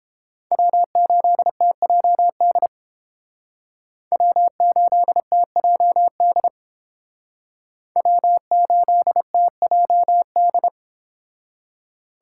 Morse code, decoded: W8TJD W8TJD W8TJD

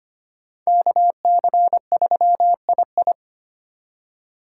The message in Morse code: KC3II

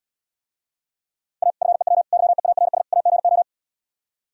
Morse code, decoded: A FEW YEARS AGO

